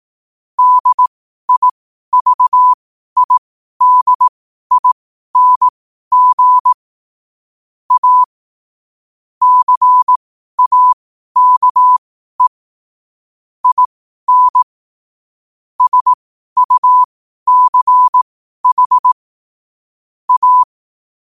Morse code, heard as a keyed tone, about 18 wpm, spaced as for 12 wpm, 1000 Hz. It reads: DIVIDING A CAKE IN SUCH A
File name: 12wpm